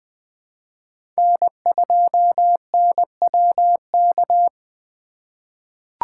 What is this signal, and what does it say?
Morse code, heard as a keyed tone, about 20 wpm, 700 Hz. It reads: N2NWK
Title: 2023 POTA Activator Call Signs - 20wpm